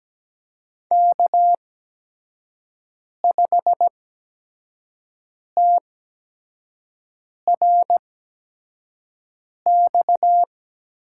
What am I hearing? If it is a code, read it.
Morse code, heard as a keyed tone, about 17 wpm, spaced as for 5 wpm, 700 Hz. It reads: K5TRX
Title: USA Call Signs - 5 Characters - 8x Character Spacing - 17wpm